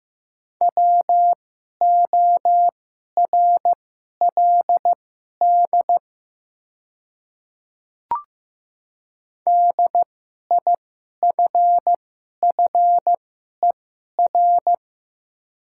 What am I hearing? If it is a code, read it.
Morse code, heard as a keyed tone, about 15 wpm, spaced as for 11 wpm, 690 Hz. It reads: WORLD DIFFER
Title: 2x Character Spacing - Top 200 Words - 15wpm